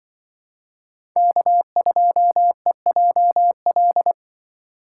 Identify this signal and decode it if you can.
Morse code, heard as a keyed tone, about 23 wpm, 700 Hz. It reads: K2EJL
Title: Speed Racing - USA 1x3 Call Signs 20wpm